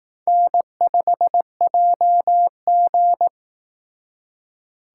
Morse code, heard as a keyed tone, about 18 wpm, 700 Hz. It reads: N5JG
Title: Speed Racing - USA 1x2 Call Signs 15wpm